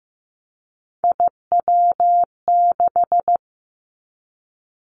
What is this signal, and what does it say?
Morse code, heard as a keyed tone, about 15 wpm, 700 Hz. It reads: IW6
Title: Three Challenging Letter-Number Combinations 15wpm